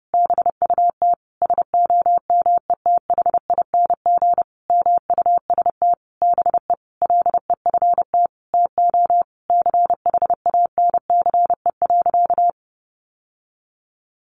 Morse code, heard as a keyed tone, about 30 wpm, 700 Hz. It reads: BUT SOMETHING MUST BE LEFT TO CHANCE.
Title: Sentences from Top 1100 Words 30wpm